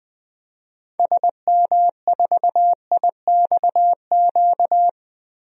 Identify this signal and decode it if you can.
Morse code, heard as a keyed tone, about 20 wpm, 700 Hz. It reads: SM4IXQ
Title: DXCC-100 Call Signs - 6 Characters - 20wpm